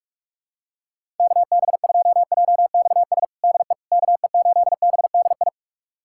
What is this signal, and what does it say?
Morse code, heard as a keyed tone, about 46 wpm, 700 Hz. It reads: KB2JXS DE KE8BDI
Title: USA Callsign DE Callsign 45wpm